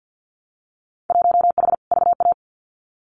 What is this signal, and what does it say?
Morse code, heard as a keyed tone, about 50 wpm, 700 Hz. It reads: 1H VA
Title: Winter Field Day Exchanges 50wpm